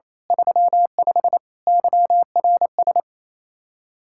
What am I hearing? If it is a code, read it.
Morse code, heard as a keyed tone, about 28 wpm, 700 Hz. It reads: 35 YRS
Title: Practice QSO - Exchange 3 - More Info - 28wpm